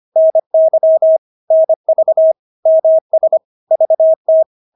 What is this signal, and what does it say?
Morse code, heard as a keyed tone, about 25 wpm, 640 Hz. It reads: NY NV MS VT
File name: State-Abbreviations-25